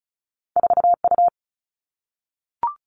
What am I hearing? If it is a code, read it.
Morse code, heard as a keyed tone, about 36 wpm, 700 Hz. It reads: 4 U
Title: Two Challenging Letter-Number Combinations 35wpm